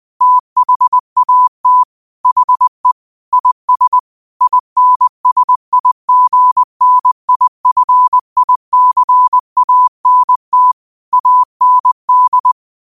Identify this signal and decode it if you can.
Morse code, heard as a keyed tone, about 20 wpm, 1000 Hz. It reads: THAT HE IS INSIGNIFICANT AND